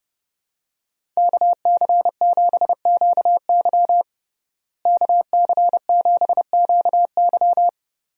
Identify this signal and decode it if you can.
Morse code, heard as a keyed tone, about 31 wpm, 700 Hz. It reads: KC7QY KC7QY
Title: USA Call Signs - 5 Characters - Repeated 2x - 30wpm